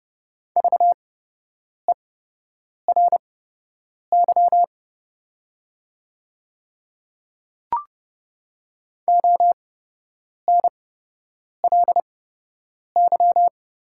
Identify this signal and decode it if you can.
Morse code, heard as a keyed tone, about 30 wpm, spaced as for 8 wpm, 700 Hz. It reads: VERY ONLY
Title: 8x Character Spacing - Top 500 Words - 30wpm